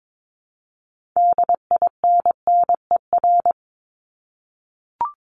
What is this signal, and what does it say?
Morse code, heard as a keyed tone, about 22 wpm, 700 Hz. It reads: DINNER
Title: Top 1751-2000 Words 22wpm